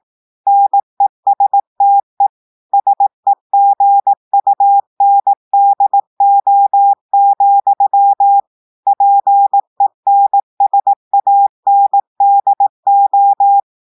Morse code, heard as a keyed tone, about 18 wpm, 800 Hz. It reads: NESTE SEGUNDO, PENSANDO
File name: audios / portugais / portugais texte en clair 1 vitesse 18 mots minutes